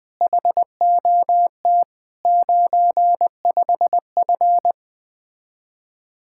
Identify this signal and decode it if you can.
Morse code, heard as a keyed tone, about 20 wpm, 700 Hz. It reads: HOT 95F
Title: Speed-Racing Common QSO Phrases Encore - 17wpm